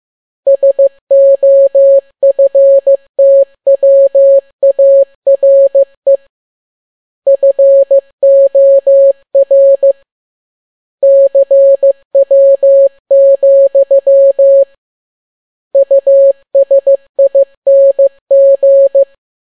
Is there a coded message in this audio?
Yes: Morse code from 0.5 to 19.0 s, heard as a keyed tone, about 15 wpm, 550 Hz. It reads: SOFTWARE FOR CW, USING